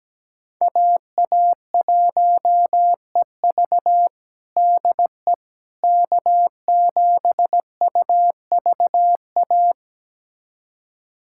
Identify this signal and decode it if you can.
Morse code, heard as a keyed tone, about 17 wpm, 700 Hz. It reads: AA1EV DE K7UVA
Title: USA Callsign DE Callsign - 5 Characters - 17wpm